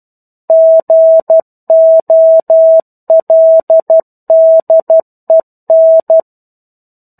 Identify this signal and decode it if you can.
Morse code, heard as a keyed tone, about 12 wpm, 650 Hz. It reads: GOLDEN
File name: 01-Chapter-01 12 WPM